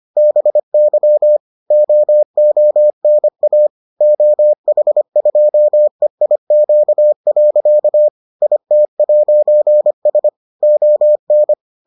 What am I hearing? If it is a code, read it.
Morse code, heard as a keyed tone, about 25 wpm, 600 Hz. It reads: BY OONA OH2EIQ. IT'S ON